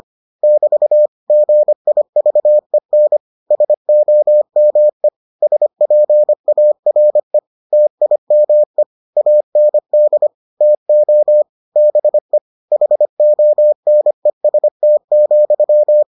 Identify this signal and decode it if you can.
Morse code, heard as a keyed tone, about 25 wpm, 600 Hz. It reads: <BT> GIVEN SOME SPARE TIME AND TO BE HONEST,